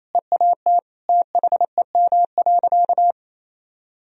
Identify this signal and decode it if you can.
Morse code, heard as a keyed tone, about 28 wpm, 700 Hz. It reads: EAT THEM.
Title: Sentences from Top 1000 Words - Review Part 4 - 28wpm